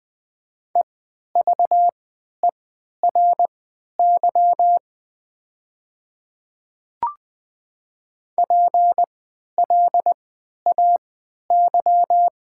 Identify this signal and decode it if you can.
Morse code, heard as a keyed tone, about 20 wpm, spaced as for 11 wpm, 700 Hz. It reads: EVERY PLAY
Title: 3x Character Spacing - Top 200 Words - 20wpm